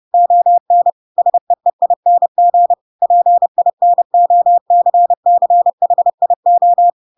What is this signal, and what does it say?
Morse code, heard as a keyed tone, about 30 wpm, 700 Hz. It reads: ON SEEING PINOCCHIO
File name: Pinocchio / Pinocchio ch7 30 30 ik1hge